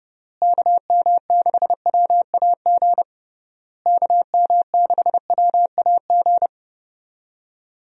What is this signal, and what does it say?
Morse code, heard as a keyed tone, about 30 wpm, 700 Hz. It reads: KM6WAG KM6WAG
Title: USA Call Signs - 6 Characters - Repeated 3x - 30wpm